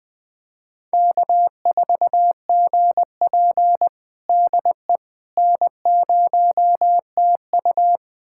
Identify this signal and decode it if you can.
Morse code, heard as a keyed tone, about 20 wpm, 700 Hz. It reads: K4GP DE N0TU
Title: USA Callsign DE Callsign - 4 Characters - 20wpm